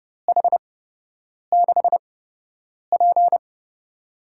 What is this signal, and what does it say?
Morse code, heard as a keyed tone, about 30 wpm, spaced as for 8 wpm, 700 Hz. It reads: H6P
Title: USA Call Signs - 4 Characters - 8x Character Spacing - 30wpm